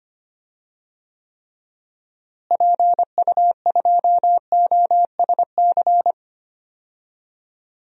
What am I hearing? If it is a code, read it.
Morse code, heard as a keyed tone, about 25 wpm, 700 Hz. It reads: PU2OSC
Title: DXCC-100 Call Signs - 6 Characters - Repeated 2x - 25wpm